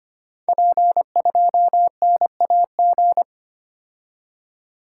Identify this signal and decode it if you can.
Morse code, heard as a keyed tone, about 25 wpm, 700 Hz. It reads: P2NAG
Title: Code Groups - 5 Letter:Numbers - 25wpm